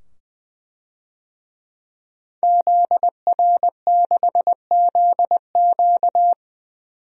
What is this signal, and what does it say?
Morse code, heard as a keyed tone, about 20 wpm, 700 Hz. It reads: ZR6ZQ
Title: DXCC-100 Call Signs - 5 Characters - 20wpm